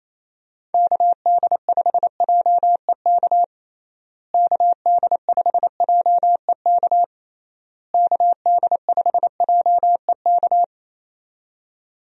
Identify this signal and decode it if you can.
Morse code, heard as a keyed tone, about 28 wpm, 700 Hz. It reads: KD5JEK KD5JEK KD5JEK
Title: USA Call Signs - 6 Characters - Repeated 3x - 28wpm